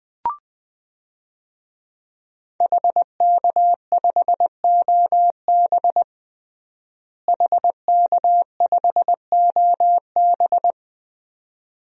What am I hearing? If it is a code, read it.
Morse code, heard as a keyed tone, about 20 wpm, 700 Hz. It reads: HK5OB HK5OB
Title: DXCC-100 Call Signs - 5 Characters - Repeated 2x - 20wpm